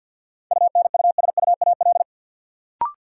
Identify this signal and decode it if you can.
Morse code, heard as a keyed tone, about 53 wpm, 700 Hz. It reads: UNUSUAL
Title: Sentences from Top 1400 Words - New Word Drill - 50wpm